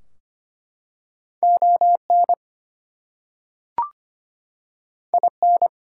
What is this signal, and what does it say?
Morse code, heard as a keyed tone, about 25 wpm, 700 Hz. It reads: ON IN
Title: Two Letter Words - 25wpm